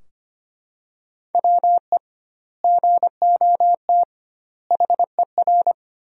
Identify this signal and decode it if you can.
Morse code, heard as a keyed tone, about 25 wpm, 700 Hz. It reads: WE GOT HER
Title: 2x Word-Spacing - Sentences from Top 400 Words - 25wpm